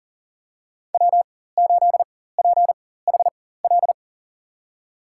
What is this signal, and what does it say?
Morse code, heard as a keyed tone, about 40 wpm, spaced as for 19 wpm, 700 Hz. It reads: W8PHL
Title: USA Call Signs - 5 Characters - 4x Character Spacing - 40wpm